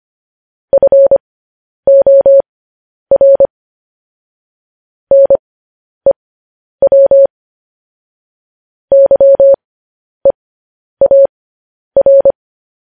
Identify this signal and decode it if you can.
Morse code, heard as a keyed tone, about 25 wpm, spaced as for 10 wpm, 550 Hz. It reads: FOR NEW YEAR
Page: News Headlines in Morse Code at 10 WPM